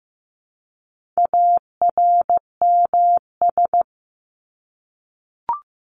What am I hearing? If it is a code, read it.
Morse code, heard as a keyed tone, about 15 wpm, 700 Hz. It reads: ARMS
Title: ES:S Suffix - Top 100 Words with Base Words 15wpm